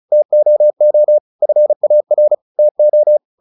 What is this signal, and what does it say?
Morse code, heard as a keyed tone, about 34 wpm, 600 Hz. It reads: TOO FAR TO
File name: the old man and the sea 2 35wpm